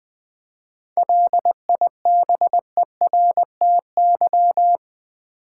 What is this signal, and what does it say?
Morse code, heard as a keyed tone, about 20 wpm, 700 Hz. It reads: LIBERTY
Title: 7 Letter Words - First 3 Letters Given - 20wpm